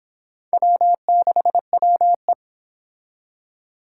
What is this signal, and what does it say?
Morse code, heard as a keyed tone, about 26 wpm, 700 Hz. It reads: W6WE